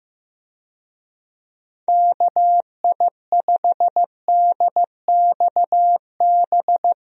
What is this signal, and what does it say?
Morse code, heard as a keyed tone, about 15 wpm, 700 Hz. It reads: KI5DXB
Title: USA Call Signs - 6 Characters - 15wpm